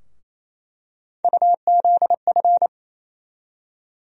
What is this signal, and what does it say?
Morse code, heard as a keyed tone, about 28 wpm, 700 Hz. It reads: UZF
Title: USA Call Sign Postfixes - 3 Letters 28wpm